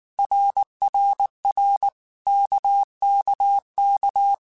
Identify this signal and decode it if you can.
Morse code, heard as a keyed tone, about 19 wpm, 800 Hz. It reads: RRR KKK
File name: Basic Letter Groups By7Groups / RKPX / RKPX@18